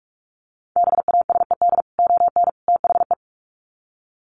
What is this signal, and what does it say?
Morse code, heard as a keyed tone, about 44 wpm, 700 Hz. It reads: BASED ON THE
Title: Sets of 3 Words 45wpm